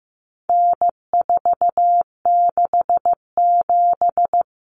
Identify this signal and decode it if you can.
Morse code, heard as a keyed tone, about 15 wpm, 700 Hz. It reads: N467
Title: Common Cut Numbers - 4-Digit 15wpm